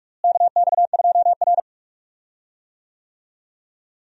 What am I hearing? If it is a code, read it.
Morse code, heard as a keyed tone, about 46 wpm, 700 Hz. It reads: KX2R
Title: USA Call Signs - 4 Characters - Repeated 3x - 45wpm